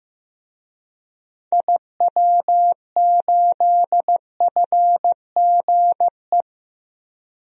Morse code, heard as a keyed tone, about 15 wpm, 700 Hz. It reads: IW8FGE
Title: DXCC-100 Call Signs - 6 Characters - Repeated 2x - 15wpm